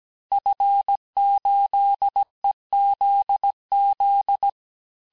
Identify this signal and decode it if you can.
Morse code, heard as a keyed tone, about 17 wpm, 800 Hz. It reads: F8EZZ
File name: corrige-cours-1581-fin-du-QSO-vitesse-16-mpm